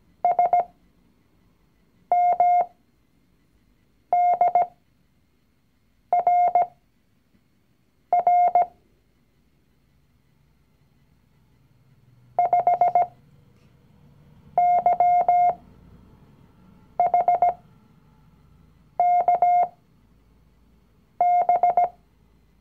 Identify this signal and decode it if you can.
Morse code, heard as a keyed tone, about 17 wpm, spaced as for 5 wpm, 710 Hz. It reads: SMDRR 5YHKB
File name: CW-Online-Kurs / Anfänger / Lektion 019